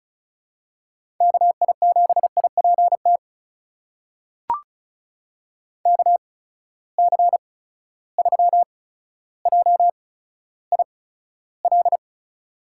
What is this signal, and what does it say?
Morse code, heard as a keyed tone, about 34 wpm, spaced as for 15 wpm, 700 Hz. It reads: KI7IPT KC3JIL
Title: USA Call Signs - 6 Characters - 8x Character Spacing - 35wpm